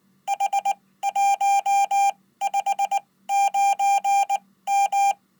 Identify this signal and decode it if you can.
Morse code, heard as a keyed tone, about 19 wpm, spaced as for 15 wpm, 800 Hz. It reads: H159M
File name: Code